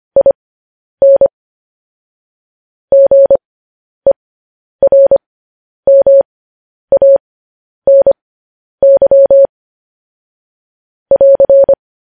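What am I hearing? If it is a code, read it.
Morse code, heard as a keyed tone, about 25 wpm, spaced as for 10 wpm, 550 Hz. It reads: IN GERMANY <AR>
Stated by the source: Morse code news headlines sent at 10.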